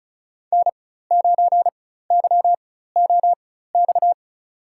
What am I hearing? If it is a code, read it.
Morse code, heard as a keyed tone, about 36 wpm, spaced as for 16 wpm, 700 Hz. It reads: N9YOX
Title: USA Call Signs - 5 Characters - 4x Character Spacing - 35wpm